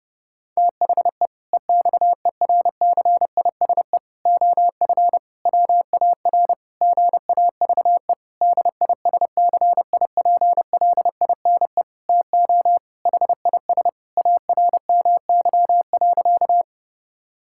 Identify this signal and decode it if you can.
Morse code, heard as a keyed tone, about 30 wpm, 700 Hz. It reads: THE EXERCISE OF WAR GAVE DISCIPLINE TO HIS ARMY.
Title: Sentences from Top 1000 Words - Review Part 6 - 30wpm